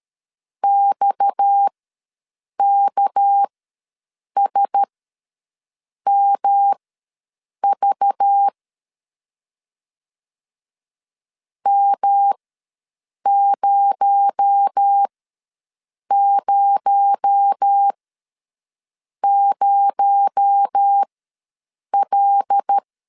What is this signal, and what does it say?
Morse code, heard as a keyed tone, about 13 wpm, spaced as for 6 wpm, 800 Hz. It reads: XKSMV M000L
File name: LEZIONI / 6 PKX0 / 6 PKX0 ws 12wpm 3448